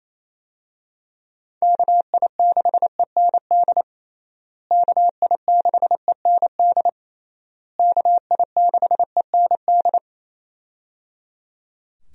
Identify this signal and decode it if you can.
Morse code, heard as a keyed tone, about 28 wpm, 700 Hz. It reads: KI6END KI6END KI6END
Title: USA Call Signs - 6 Characters - Repeated 3x - 28wpm